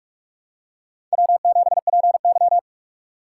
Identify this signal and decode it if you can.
Morse code, heard as a keyed tone, about 46 wpm, 700 Hz. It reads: W7PY
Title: USA Call Signs - 4 Characters - 6x Character Spacing - 45wpm